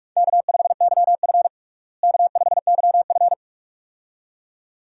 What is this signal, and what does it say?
Morse code, heard as a keyed tone, about 42 wpm, 700 Hz. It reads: K5YF K5YF
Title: USA Call Signs - 4 Characters - Repeated 2x - 45wpm